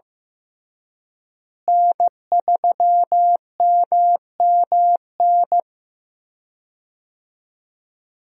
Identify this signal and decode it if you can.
Morse code, heard as a keyed tone, about 15 wpm, 700 Hz. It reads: N3MMN
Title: USA Call Signs - 5 Characters - Repeated 3x - 15wpm